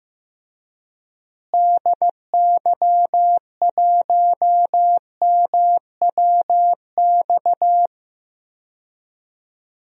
Morse code, heard as a keyed tone, about 16 wpm, 700 Hz. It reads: DY1MWX
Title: DXCC-100 Call Signs - 6 Characters - Repeated 3x - 15wpm